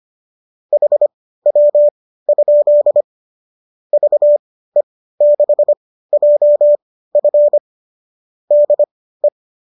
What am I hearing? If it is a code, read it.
Morse code, heard as a keyed tone, about 25 wpm, spaced as for 15 wpm, 590 Hz. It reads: HW? VE6JF DE
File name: qso203 15